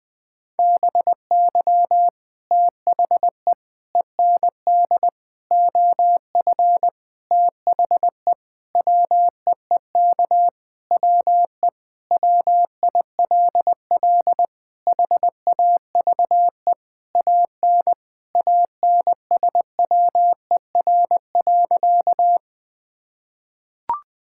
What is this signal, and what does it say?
Morse code, heard as a keyed tone, about 20 wpm, 700 Hz. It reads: BY THE END OF THE WEEK WE WILL HAVE AN ANSWER.
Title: Sentences with 5-Word Phrases - 20wpm